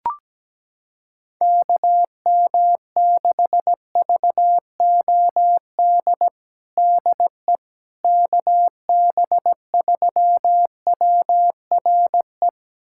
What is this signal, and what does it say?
Morse code, heard as a keyed tone, about 17 wpm, 700 Hz. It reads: KM6VOD DE KB3WRE
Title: USA Callsign DE Callsign 17wpm